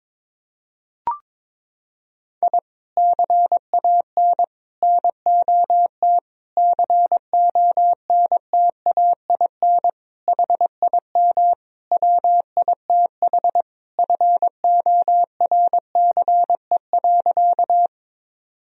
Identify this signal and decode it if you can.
Morse code, heard as a keyed tone, about 22 wpm, 700 Hz. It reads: I CAN NOT CONTAIN HIM WITH FORCE.
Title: Sentences from Top 1000 Words - Review Part 3 - 22wpm